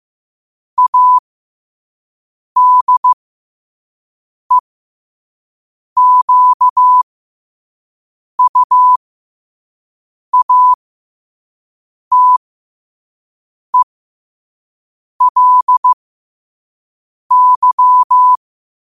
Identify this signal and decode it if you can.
Morse code, heard as a keyed tone, about 15 wpm, spaced as for 5 wpm, 1000 Hz. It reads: ADEQUATELY